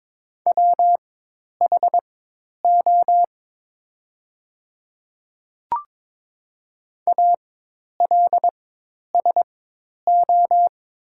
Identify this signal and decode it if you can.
Morse code, heard as a keyed tone, about 22 wpm, spaced as for 10 wpm, 700 Hz. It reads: WHO ALSO
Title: 4x Character Spacing - Top 300 Words - 22wpm